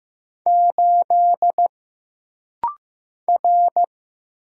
Morse code, heard as a keyed tone, about 15 wpm, 700 Hz. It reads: 8 R
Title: Single Letter-Number Alt - Rapid-Fire 15wpm